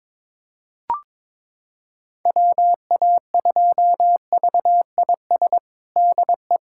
Morse code, heard as a keyed tone, about 22 wpm, 700 Hz. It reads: WA2VIS DE
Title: USA Callsign DE Callsign - 6 Characters - 22wpm